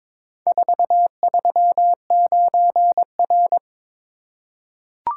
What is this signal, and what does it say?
Morse code, heard as a keyed tone, about 22 wpm, 700 Hz. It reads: 439R
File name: practice / 22wpm / Four Letter-Number Combinations - 22wpm